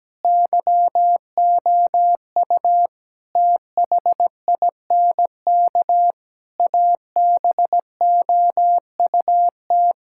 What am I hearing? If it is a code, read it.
Morse code, heard as a keyed tone, about 17 wpm, 700 Hz. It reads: YOU THINK ABOUT
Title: Sentences from Top 1000 Words - Review Part 6 - 17wpm